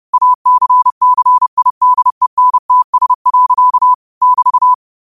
Morse code, heard as a keyed tone, about 30 wpm, 1000 Hz. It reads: ACCIDENTS. <BT>